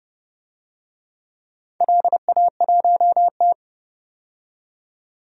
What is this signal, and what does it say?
Morse code, heard as a keyed tone, about 30 wpm, 700 Hz. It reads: LA1T
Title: DXCC-100 Call Signs - 4 Characters - 30wpm